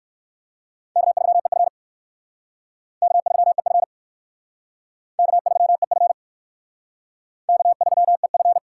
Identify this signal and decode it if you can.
Morse code, heard as a keyed tone, about 55 wpm, 700 Hz. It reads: K3EF K3EF K3EF K3EF
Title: Speed Racing - USA 1x2 Call Signs 45wpm